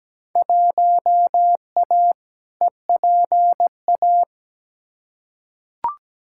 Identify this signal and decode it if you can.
Morse code, heard as a keyed tone, about 17 wpm, 700 Hz. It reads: 1A EPA
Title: Field Day Exchanges - 17wpm